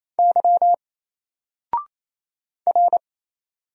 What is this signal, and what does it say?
Morse code, heard as a keyed tone, about 27 wpm, 700 Hz. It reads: Y R
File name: Single-Letter-Number-28wpm